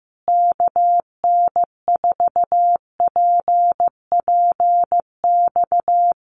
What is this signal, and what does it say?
Morse code, heard as a keyed tone, about 14 wpm, 700 Hz. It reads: KN4PPX
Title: Common QSO Phrases Encore 15wpm